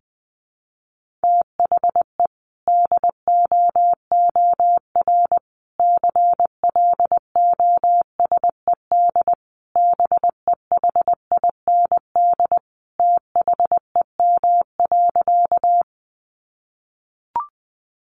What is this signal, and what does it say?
Morse code, heard as a keyed tone, about 20 wpm, 700 Hz. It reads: THE DOOR CLOSED BEHIND THEM.
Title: Sentences from Top 1400 Words 20wpm